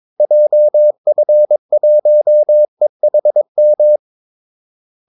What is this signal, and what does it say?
Morse code, heard as a keyed tone, about 22 wpm, 600 Hz. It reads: JF1EHM
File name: Callsigns-22wpm